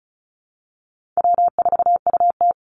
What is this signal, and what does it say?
Morse code, heard as a keyed tone, about 34 wpm, 700 Hz. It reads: W4UT
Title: Call Signs - 35wpm